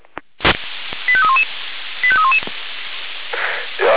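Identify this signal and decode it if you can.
Five-tone selective call (ZVEI), heard as quick runs of five tones, tones of about 70 ms. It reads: 85311 85311
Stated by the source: ZVEI five-tone selective call received on SDR.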